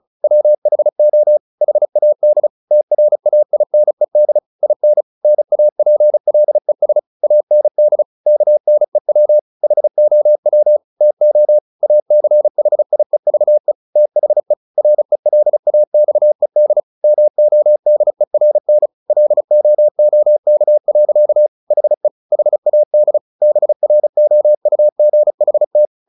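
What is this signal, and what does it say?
Morse code, heard as a keyed tone, about 34 wpm, 600 Hz. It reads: WHO HAD TRAINED IN NAPLES AND KNEW HOW TO ACHIEVE THE RELAXED MODERN LOOK. HE HAD BROUGHT
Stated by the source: Morse practice at 35 wpm from CW med Gnister, sending Ken Follett's novel Never in English.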